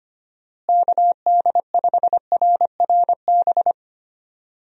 Morse code, heard as a keyed tone, about 25 wpm, 700 Hz. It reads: KD5RRB
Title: USA Call Signs - 6 Characters - Repeated 2x - 25wpm